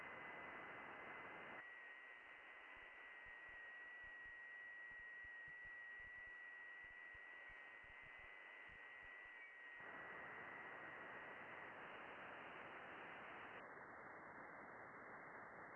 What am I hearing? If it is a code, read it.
FT8.
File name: contests / june-vhf-k5tr- -0200